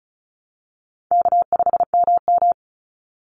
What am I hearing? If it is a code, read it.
Morse code, heard as a keyed tone, about 34 wpm, 700 Hz. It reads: K5MM
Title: Call Signs Repeated 2x - 35wpm